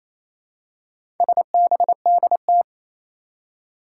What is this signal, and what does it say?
Morse code, heard as a keyed tone, about 28 wpm, 700 Hz. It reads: SBDT
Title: Four Challenging Letter-Number Combinations 28wpm